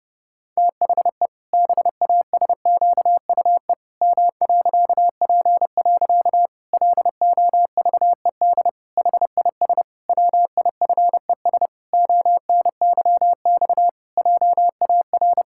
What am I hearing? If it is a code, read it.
Morse code, heard as a keyed tone, about 30 wpm, 700 Hz. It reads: THE BASQUE M.P. LOVED HIS WIFES ONYX JAR
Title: Pangram Challenge v2 - 30wpm